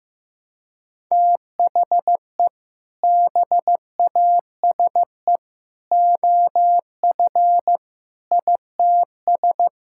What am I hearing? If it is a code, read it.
Morse code, heard as a keyed tone, about 16 wpm, 700 Hz. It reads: THE BASE OF ITS
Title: Sentences from Top 1000 Words - Review Part 5 - 15wpm